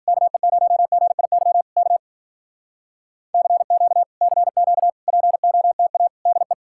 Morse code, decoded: KE0GIY K CQ CQ POTA DE

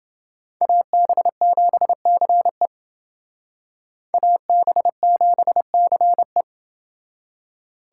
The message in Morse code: AB7CE AB7CE

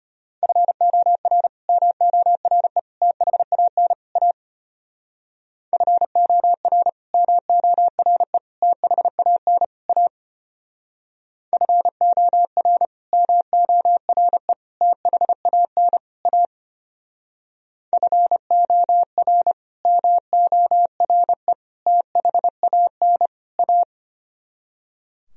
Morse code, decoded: FOR MORE THAN A FOR MORE THAN A FOR MORE THAN A FOR MORE THAN A

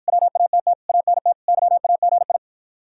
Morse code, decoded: WATT ANT YAGI

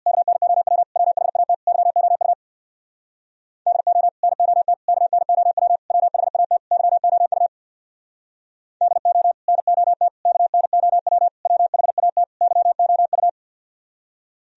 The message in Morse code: KNOW WHAT YOU DO NOT KNOW WHAT YOU DO NOT KNOW WHAT YOU

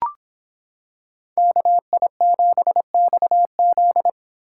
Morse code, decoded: KI7XZ

